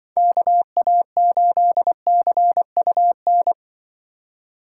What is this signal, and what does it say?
Morse code, heard as a keyed tone, about 24 wpm, 700 Hz. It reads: KA8CUN
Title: Speed Racing - USA 2x3 Call Signs 20wpm